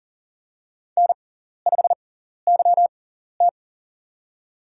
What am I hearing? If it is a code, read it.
Morse code, heard as a keyed tone, about 42 wpm, spaced as for 14 wpm, 700 Hz. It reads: N5YT
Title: USA Call Signs - 4 Characters - 6x Character Spacing - 40wpm